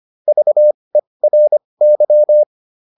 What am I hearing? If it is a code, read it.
Morse code, heard as a keyed tone, about 25 wpm, spaced as for 20 wpm, 600 Hz. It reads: VERY